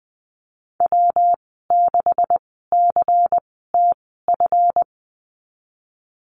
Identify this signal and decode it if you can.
Morse code, heard as a keyed tone, about 20 wpm, spaced as for 15 wpm, 700 Hz. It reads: W6CTF
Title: 147 Lesson 24g - Letter y - Call Signs 20wpm